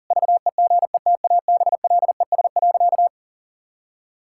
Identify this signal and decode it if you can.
Morse code, heard as a keyed tone, about 40 wpm, 700 Hz. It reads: VEGETABLES.
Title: Pangram Challenge v2 - 40wpm